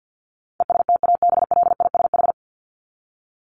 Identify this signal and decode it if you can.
Morse code, heard as a keyed tone, about 50 wpm, 700 Hz. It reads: ESTABLISH
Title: ED Suffix - Top 100 Words with Base Words 50wpm